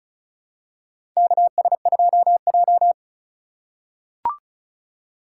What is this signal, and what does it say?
Morse code, heard as a keyed tone, about 36 wpm, 700 Hz. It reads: KS2J
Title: USA Call Signs - 4 Characters - 4x Character Spacing - 35wpm